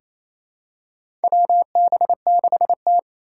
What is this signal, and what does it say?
Morse code, heard as a keyed tone, about 28 wpm, 700 Hz. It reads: WB6T
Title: USA Call Signs - 3x Character Spacing - 28wpm